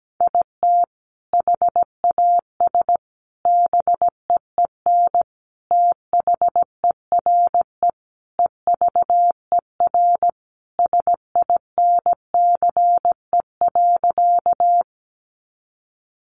Morse code, heard as a keyed tone, about 17 wpm, 700 Hz. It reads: IT HAS BEEN THERE EVER SINCE.
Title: Sentences from Top 400 Words 17wpm